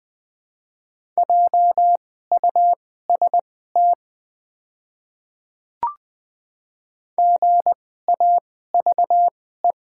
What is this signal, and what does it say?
Morse code, heard as a keyed tone, about 20 wpm, spaced as for 15 wpm, 700 Hz. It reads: JUST GAVE
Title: 2x Character Spacing - Top 500 Words - 20wpm